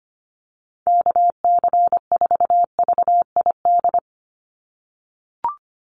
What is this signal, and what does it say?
Morse code, heard as a keyed tone, about 25 wpm, 700 Hz. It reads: KC4VID
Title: USA Call Signs 25wpm